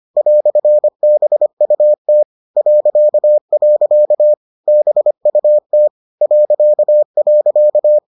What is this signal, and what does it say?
Morse code, heard as a keyed tone, about 25 wpm, 600 Hz. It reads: "BUT .. BUT ..